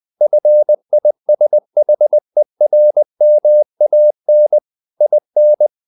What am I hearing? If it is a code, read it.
Morse code, heard as a keyed tone, about 20 wpm, 600 Hz. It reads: FISHERMAN IN
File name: the old man and the sea / the old man and the sea part 1 20wpm